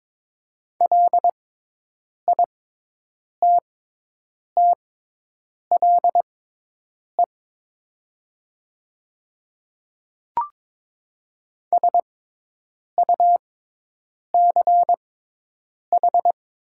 Morse code, heard as a keyed tone, about 22 wpm, spaced as for 8 wpm, 700 Hz. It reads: LITTLE SUCH
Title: 6x Character Spacing - Top 300 Words - 22wpm